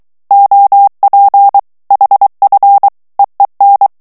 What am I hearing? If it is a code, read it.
Morse code, heard as a keyed tone, about 23 wpm, 800 Hz. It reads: OP HF EEN
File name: HOLLANDAIS texte clair 1 vitesse 22 mots minute